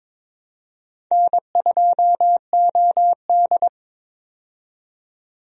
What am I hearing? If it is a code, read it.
Morse code, heard as a keyed tone, about 22 wpm, 700 Hz. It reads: N2OD
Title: USA Call Signs - 4 Characters - 22wpm